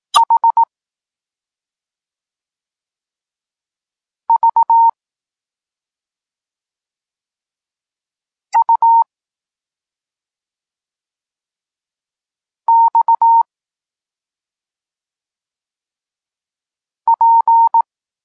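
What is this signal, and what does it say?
Morse code, heard as a keyed tone, about 18 wpm, 920 Hz. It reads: H V U X P
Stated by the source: Polish Morse lesson teaching the letter C, from SP9MOA Niepołomicki Klub Krótkofalowców.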